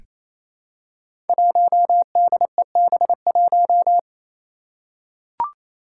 Morse code, heard as a keyed tone, about 27 wpm, 700 Hz. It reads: 1DEB1
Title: Five Challenging Letter-Number Combinations 28wpm